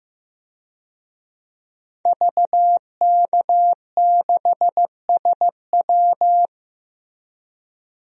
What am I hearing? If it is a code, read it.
Morse code, heard as a keyed tone, about 15 wpm, 700 Hz. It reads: VK6SW
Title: DXCC-100 Call Signs - 5 Characters - Repeated 2x - 15wpm